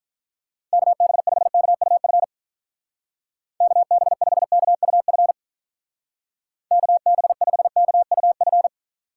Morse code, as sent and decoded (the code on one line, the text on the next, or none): -.- -... ..... -.- ..- ..-. / -.- -... ..... -.- ..- ..-. / -.- -... ..... -.- ..- ..-.
KB5KUF KB5KUF KB5KUF